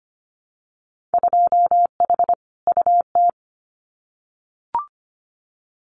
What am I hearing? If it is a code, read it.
Morse code, heard as a keyed tone, about 25 wpm, 690 Hz. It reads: 2H UT
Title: Winter Field Day Exchanges Encore 25wpm